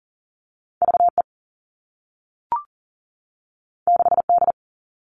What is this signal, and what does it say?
Morse code, heard as a keyed tone, about 40 wpm, 700 Hz. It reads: VE 6D
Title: All Cut Numbers - 2-Digit 40wpm